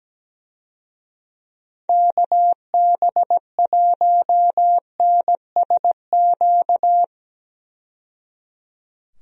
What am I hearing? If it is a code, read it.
Morse code, heard as a keyed tone, about 17 wpm, 700 Hz. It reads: KB1NSQ
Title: USA Call Signs - 6 Characters - 17wpm